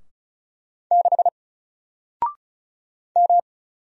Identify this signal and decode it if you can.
Morse code, heard as a keyed tone, about 36 wpm, 700 Hz. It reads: 6 M
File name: Single-Letter-Number-35wpm